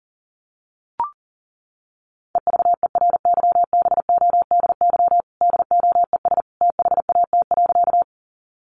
Morse code, heard as a keyed tone, about 40 wpm, 700 Hz. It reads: EVERYBODY DOES THAT.
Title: Sentences from Top 1400 Words 40wpm